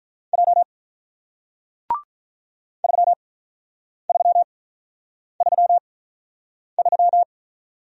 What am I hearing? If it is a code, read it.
Morse code, heard as a keyed tone, about 45 wpm, 700 Hz. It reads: J 3 3 3 3